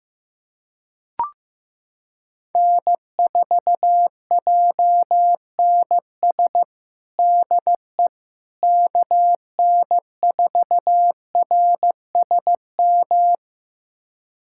Morse code, heard as a keyed tone, about 15 wpm, 700 Hz. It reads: N4JNS DE KN4RSM
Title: USA Callsign DE Callsign 15wpm